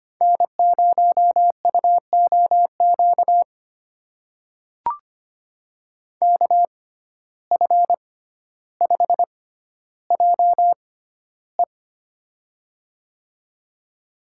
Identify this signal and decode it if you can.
Morse code, heard as a keyed tone, about 25 wpm, spaced as for 13 wpm, 700 Hz. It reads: N0UOQ KF5JE
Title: USA Call Signs - 6x Character Spacing - 25wpm